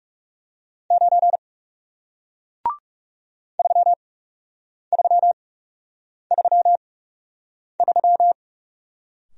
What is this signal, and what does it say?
Morse code, heard as a keyed tone, about 40 wpm, spaced as for 8 wpm, 700 Hz. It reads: 9 3333